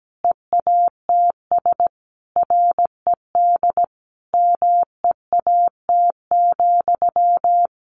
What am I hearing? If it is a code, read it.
Morse code, heard as a keyed tone, about 17 wpm, 700 Hz. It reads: EATS RED MEAT,
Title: Easy Phrases 17wpm